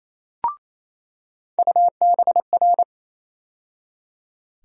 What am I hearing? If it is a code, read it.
Morse code, heard as a keyed tone, about 28 wpm, 700 Hz. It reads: UBR
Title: USA Call Sign Postfixes - 3 Letters 28wpm